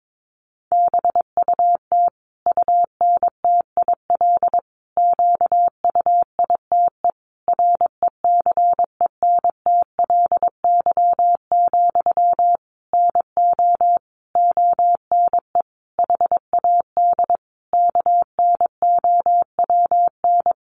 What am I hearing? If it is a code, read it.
Morse code, heard as a keyed tone, about 22 wpm, 700 Hz. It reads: BUT UNTIL QUITE RECENTLY, NO ONE HAD KNOWN
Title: Sentences from Top 1200 Words 22wpm